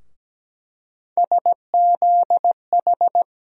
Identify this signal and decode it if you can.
Morse code, heard as a keyed tone, about 17 wpm, 700 Hz. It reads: SZH
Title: USA Call Sign Postfixes - 3 Letters 17wpm